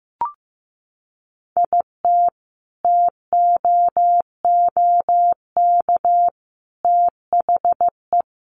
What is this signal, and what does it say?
Morse code, heard as a keyed tone, about 15 wpm, 700 Hz. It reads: IT TOOK THE
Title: Sentences from Top 1300 Words 15wpm